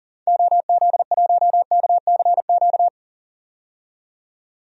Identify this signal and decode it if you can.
Morse code, heard as a keyed tone, about 40 wpm, 700 Hz. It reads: OZ1KCQ
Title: DXCC-100 Call Signs - 6 Characters - 40wpm